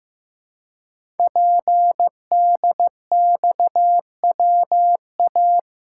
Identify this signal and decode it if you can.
Morse code, heard as a keyed tone, about 15 wpm, 700 Hz. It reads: PDXWA
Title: Code Groups - 5 Letter:Numbers - 15wpm